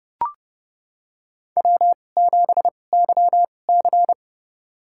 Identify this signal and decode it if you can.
Morse code, heard as a keyed tone, about 32 wpm, spaced as for 22 wpm, 700 Hz. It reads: W7YC
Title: USA Call Signs - 4 Characters - 2x Character Spacing - 30wpm